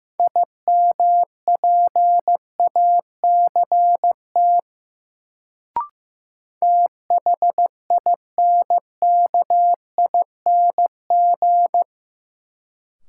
Morse code, heard as a keyed tone, about 15 wpm, 700 Hz. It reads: IMPACT THINKING
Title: Sentences from Top 1000 Words - Review Part 6 - New Word Drill - 15wpm